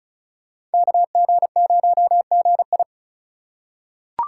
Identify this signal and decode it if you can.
Morse code, heard as a keyed tone, about 36 wpm, 700 Hz. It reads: KG0GI
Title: USA Call Signs - 5 Characters - 2x Character Spacing - 35wpm